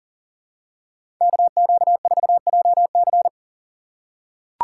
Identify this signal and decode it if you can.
Morse code, heard as a keyed tone, about 40 wpm, 700 Hz. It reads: KQ4JC
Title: USA Call Signs - 2x Character Spacing - 40wpm